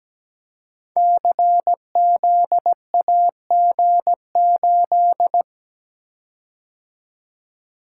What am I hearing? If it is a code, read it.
Morse code, heard as a keyed tone, about 17 wpm, 700 Hz. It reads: CZAG8
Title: Code Groups - 5 Letter:Numbers - 17wpm